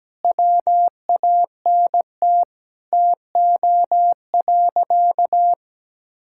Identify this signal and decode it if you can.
Morse code, heard as a keyed tone, about 17 wpm, 700 Hz. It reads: WANT TO.
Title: Sentences from Top 1600 Words 17wpm